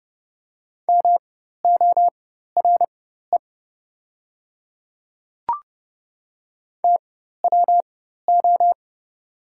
Morse code, heard as a keyed tone, about 30 wpm, spaced as for 14 wpm, 700 Hz. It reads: MORE TWO
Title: 4x Character Spacing - Top 100 Words - 30wpm